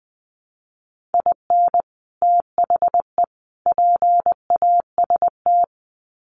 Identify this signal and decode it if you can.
Morse code, heard as a keyed tone, about 20 wpm, 700 Hz. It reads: IN THE PAST
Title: Sets of 3 Words 20wpm